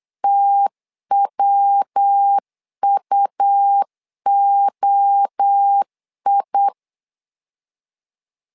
Morse code, heard as a keyed tone, about 8 wpm, 800 Hz. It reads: TWUOI
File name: LEZIONI / 4 Q 8wpm 2732